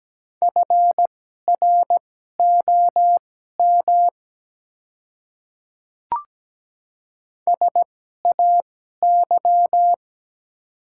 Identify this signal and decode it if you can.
Morse code, heard as a keyed tone, about 17 wpm, spaced as for 12 wpm, 690 Hz. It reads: FROM SAY
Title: 2x Character Spacing - Top 300 Words - 17wpm